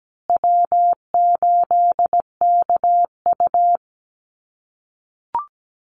Morse code, heard as a keyed tone, about 17 wpm, 700 Hz. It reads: W8KU
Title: USA Call Signs Repeated 2x 17wpm